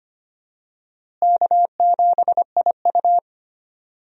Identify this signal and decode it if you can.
Morse code, heard as a keyed tone, about 25 wpm, 700 Hz. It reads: K7IU
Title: USA Call Signs - 4 Characters - Repeated 2x - 25wpm